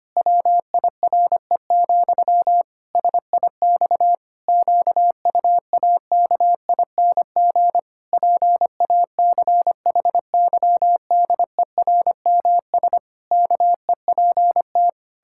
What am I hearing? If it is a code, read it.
Morse code, heard as a keyed tone, about 25 wpm, 700 Hz. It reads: WIRE, SIX QUAKING PACHYDERMS KEPT
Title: Pangram Challenge v2 - 25wpm